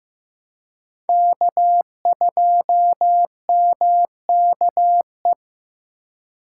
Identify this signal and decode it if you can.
Morse code, heard as a keyed tone, about 18 wpm, 700 Hz. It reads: K2MKE
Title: Speed Racing - USA 1x3 Call Signs 15wpm